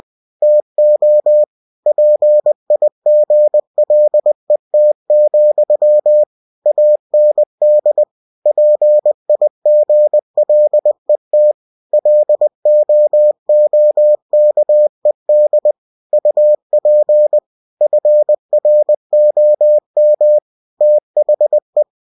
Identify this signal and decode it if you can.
Morse code, heard as a keyed tone, about 20 wpm, 600 Hz. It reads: TO PIGLET, AND PIGLET LOOKED UP FROM THE
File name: cw-books / winnie-the-pooh / winnie-20wpm-page-0038